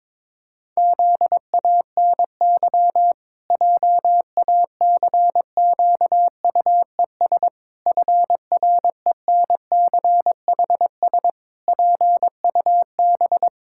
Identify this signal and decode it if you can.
Morse code, heard as a keyed tone, about 22 wpm, 700 Hz. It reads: ZANY JACQUES FRENCHS PUB
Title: Pangram Challenge v2 - 22wpm